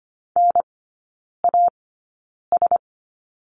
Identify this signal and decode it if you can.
Morse code, heard as a keyed tone, about 25 wpm, spaced as for 9 wpm, 700 Hz. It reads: NAS
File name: Lesson 14e - Number 2 - Three Characters 25wpm